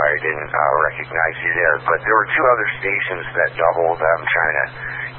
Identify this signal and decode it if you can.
Emergency Alert System (SAME) tones, heard as harsh buzzing bursts, with speech. It is heard from 1.0 to 4.9 s.